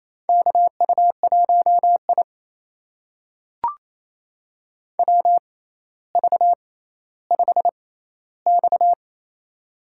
Morse code, heard as a keyed tone, about 28 wpm, spaced as for 14 wpm, 700 Hz. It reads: KU1I WV5X
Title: USA Call Signs - 6x Character Spacing - 28wpm